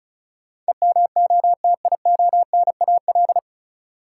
Morse code, heard as a keyed tone, about 36 wpm, 700 Hz. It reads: EMOTIONAL
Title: Sentences from Top 1000 Words - Review Part 5 - New Word Drill - 35wpm